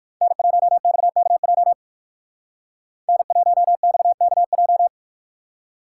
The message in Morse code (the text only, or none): N1XKJ N1XKJ